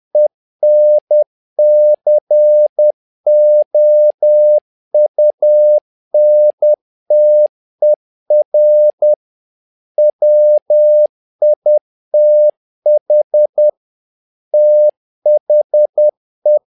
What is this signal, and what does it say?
Morse code, heard as a keyed tone, about 10 wpm, 600 Hz. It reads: ENCOUNTER WITH THE
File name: learning center / tower trouble / tower trouble 10wpm